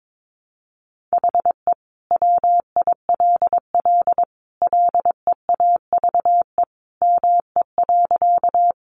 Morse code, heard as a keyed tone, about 22 wpm, 700 Hz. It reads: HE WILL LEAVE ME.
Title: Sentences from Top 400 Words 22wpm